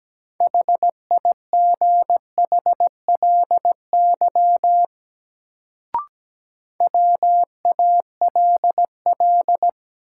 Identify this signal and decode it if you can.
Morse code, heard as a keyed tone, about 17 wpm, 700 Hz. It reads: HIGHLY WALL
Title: Sentences from Top 1100 Words - New Word Drill Alt - 17wpm